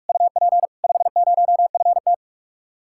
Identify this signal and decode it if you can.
Morse code, heard as a keyed tone, about 57 wpm, 700 Hz. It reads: UP 50FT